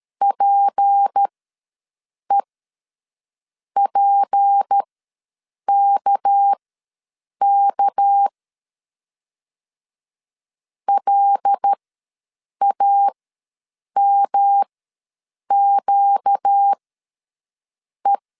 Morse code, heard as a keyed tone, about 13 wpm, spaced as for 6 wpm, 800 Hz. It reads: PEPKK LAMQE
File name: LEZIONI / 6 PKX0 / 6 PK ws 12wpm 3116